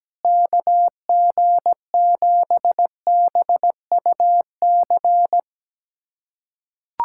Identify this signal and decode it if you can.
Morse code, heard as a keyed tone, about 17 wpm, 700 Hz. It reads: KG7BUC
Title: USA Callsign DE Callsign 17wpm